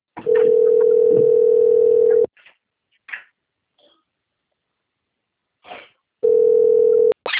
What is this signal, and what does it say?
Ringback tone (North America), 440 + 480 Hz.